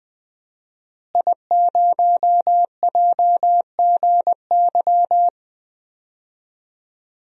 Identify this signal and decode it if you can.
Morse code, heard as a keyed tone, about 20 wpm, 700 Hz. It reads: I0JGY
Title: DXCC-100 Call Signs - 5 Characters - Repeated 2x - 20wpm